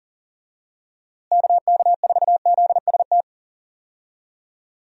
Morse code, heard as a keyed tone, about 40 wpm, 700 Hz. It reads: KK4ZST
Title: USA Call Signs - 6 Characters - Repeated 3x - 40wpm